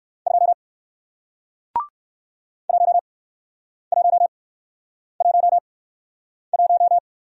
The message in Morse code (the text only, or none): 2 1 1 1 1